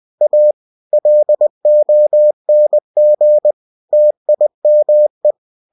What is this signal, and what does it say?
Morse code, heard as a keyed tone, about 20 wpm, 600 Hz. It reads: A LONG TIME
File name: cw-books / winnie-the-pooh / winnie-20wpm-page-0048